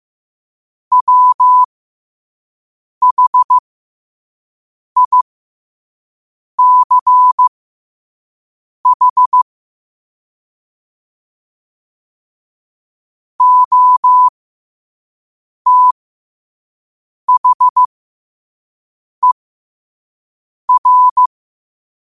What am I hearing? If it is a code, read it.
Morse code, heard as a keyed tone, about 15 wpm, spaced as for 5 wpm, 1000 Hz. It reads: WHICH OTHER
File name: qotdfiles / 5wpm